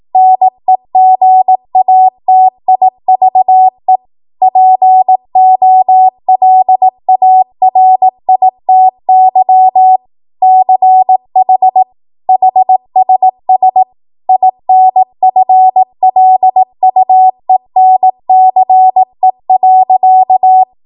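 Morse code, heard as a keyed tone, about 18 wpm, 750 Hz. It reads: NEGATIVE POLARITY CH HSS INFLUENCE.
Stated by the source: ARRL 18 wpm Morse code practice archive transmission.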